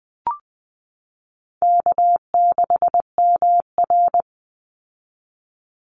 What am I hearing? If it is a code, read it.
Morse code, heard as a keyed tone, about 20 wpm, 700 Hz. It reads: K6MR
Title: Easy Call Signs - 20wpm